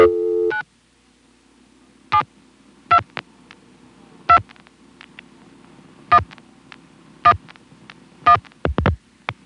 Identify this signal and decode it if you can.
DTMF, heard as telephone touch tones, tones of about 90 ms. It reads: 9 72 2 1 1 1